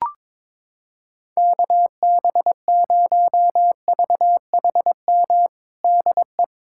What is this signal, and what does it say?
Morse code, heard as a keyed tone, about 22 wpm, 700 Hz. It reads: KB0VHM DE
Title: USA Callsign DE Callsign - 6 Characters - 22wpm